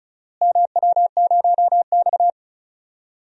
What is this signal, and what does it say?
Morse code, heard as a keyed tone, about 34 wpm, 700 Hz. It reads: MW0X
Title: DXCC-100 Call Signs - 4 Characters - Repeated 2x - 35wpm